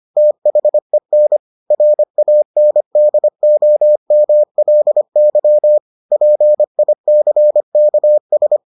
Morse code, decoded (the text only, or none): THEN RANDOMLY PICKS